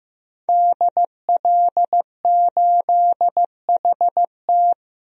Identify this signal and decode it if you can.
Morse code, heard as a keyed tone, about 15 wpm, 700 Hz. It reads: DL8HT
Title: DXCC-100 Call Signs - 5 Characters - Repeated 3x - 15wpm